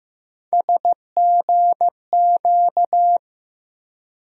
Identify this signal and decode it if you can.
Morse code, heard as a keyed tone, about 15 wpm, 700 Hz. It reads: SGQ
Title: USA Call Sign Postfixes - 3 Letters 15wpm